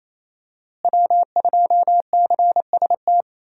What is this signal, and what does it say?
Morse code, heard as a keyed tone, about 27 wpm, 700 Hz. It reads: W2CST
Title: USA Call Signs - 5 Characters - Repeated 2x - 28wpm